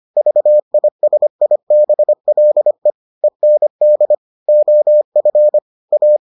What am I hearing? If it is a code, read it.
Morse code, heard as a keyed tone, about 25 wpm, 600 Hz. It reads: VISIBLE END OF A